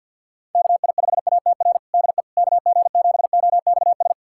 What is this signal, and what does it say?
Morse code, heard as a keyed tone, about 52 wpm, 700 Hz. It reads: KI5ATR DE KG7OXS
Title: USA Callsign DE Callsign 50wpm